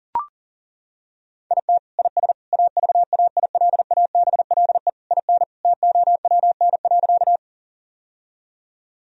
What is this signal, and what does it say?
Morse code, heard as a keyed tone, about 40 wpm, 700 Hz. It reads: IT IS AVAILABLE IN TOWN.
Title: Sentences from Top 1000 Words - Review Part 3 - 40wpm